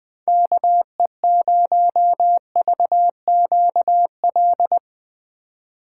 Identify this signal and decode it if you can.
Morse code, heard as a keyed tone, about 20 wpm, 700 Hz. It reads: KE0VQL
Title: USA Call Signs - 6x Character Spacing - 20wpm